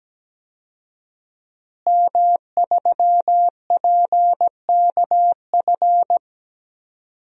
Morse code, heard as a keyed tone, about 17 wpm, 700 Hz. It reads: M3PKF
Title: DXCC-100 Call Signs - 5 Characters - Repeated 3x - 17wpm